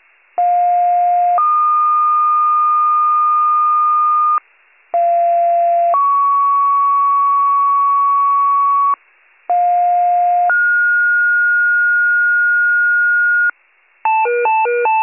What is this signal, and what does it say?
Two-tone page, heard as a one-second tone, then a three-second tone, about 710 then 1220 Hz, about 690 then 1120 Hz, about 710 then 1430 Hz. It is heard from 0.4 to 13.5 s.